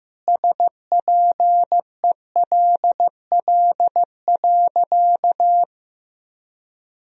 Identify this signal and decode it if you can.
Morse code, heard as a keyed tone, about 15 wpm, 700 Hz. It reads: SPELL.
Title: Sentences from Top 1000 Words - Review Part 1 - 15wpm